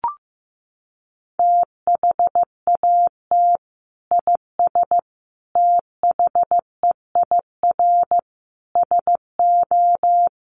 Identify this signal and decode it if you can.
Morse code, heard as a keyed tone, about 15 wpm, 700 Hz. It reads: THAT IS THEIR SO
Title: Sentences from Top 1300 Words 15wpm